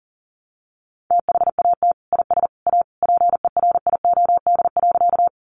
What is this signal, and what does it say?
Morse code, heard as a keyed tone, about 40 wpm, 700 Hz. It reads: THAT IS A PERIOD.
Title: Sentences from Top 800 Words 40wpm